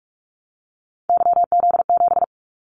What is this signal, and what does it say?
Morse code, heard as a keyed tone, about 44 wpm, 700 Hz. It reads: YZ7
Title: Callsign Prefixes for Top 25 DXCC - 45wpm